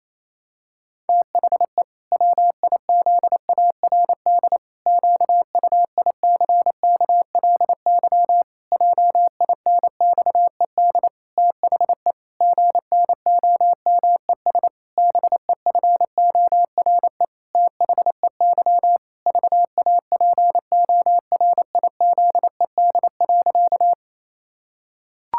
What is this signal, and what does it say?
Morse code, heard as a keyed tone, about 28 wpm, 700 Hz. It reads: THE WIZARD QUICKLY JINXED THE GNOMES BEFORE THEY VAPORIZED.
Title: Pangram Challenge v2 - 28wpm